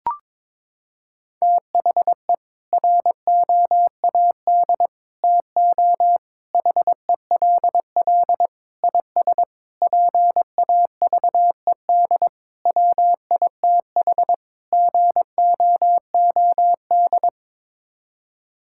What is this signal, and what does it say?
Morse code, heard as a keyed tone, about 21 wpm, 700 Hz. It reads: THE ROAD TO HELL IS PAVED WITH GOOD
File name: practice / 22wpm / Sets of 8 Words - 22wpm